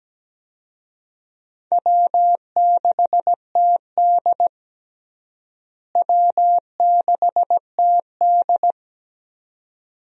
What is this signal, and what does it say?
Morse code, heard as a keyed tone, about 17 wpm, 700 Hz. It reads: W6TD W6TD
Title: USA Call Signs - 4 Characters - Repeated 2x - 17wpm